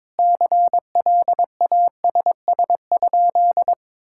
Morse code, heard as a keyed tone, about 22 wpm, 700 Hz. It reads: CLASS?
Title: Sentences from Top 1000 Words - Review Part 5 - 22wpm